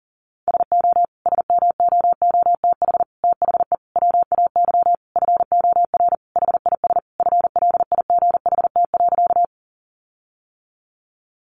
Morse code, heard as a keyed tone, about 40 wpm, 700 Hz. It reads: SO SMOOTH THE WAY FOR HIS FLIGHT.
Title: Sentences from Top 1400 Words 40wpm